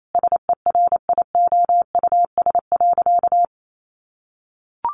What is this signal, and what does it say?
Morse code, heard as a keyed tone, about 28 wpm, 700 Hz. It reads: SERIOUS.
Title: Sentences from Top 1100 Words 28wpm